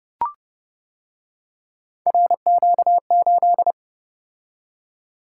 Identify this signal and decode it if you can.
Morse code, heard as a keyed tone, about 29 wpm, 700 Hz. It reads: RQ8